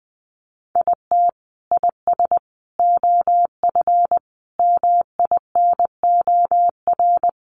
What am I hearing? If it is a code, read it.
Morse code, heard as a keyed tone, about 20 wpm, 700 Hz. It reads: IT IS OF MINOR
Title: Sentences from Top 1500 Words 20wpm